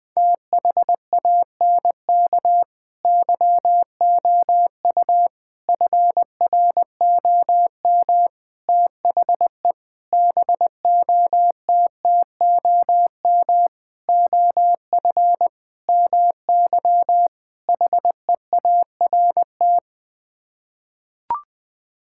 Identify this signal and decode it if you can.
Morse code, heard as a keyed tone, about 20 wpm, 700 Hz. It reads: THANK YOU FROM THE BOTTOM OF MY HEART
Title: Sets of 8 Words - 20wpm